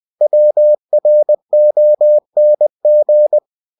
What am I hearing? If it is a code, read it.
Morse code, heard as a keyed tone, about 20 wpm, 600 Hz. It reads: WRONG
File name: cw-books / winnie-the-pooh / winnie-20wpm-page-0009